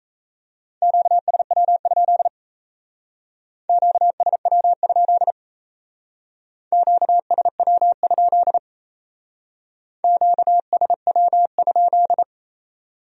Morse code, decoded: QSW? QSW? QSW? QSW?

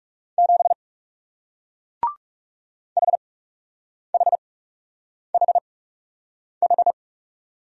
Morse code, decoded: 7 HHHH

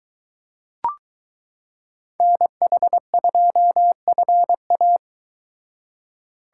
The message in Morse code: NH2FA

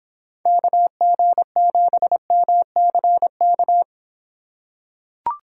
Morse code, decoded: KG7MCK

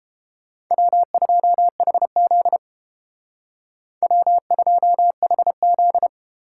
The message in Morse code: W2HZ W2HZ

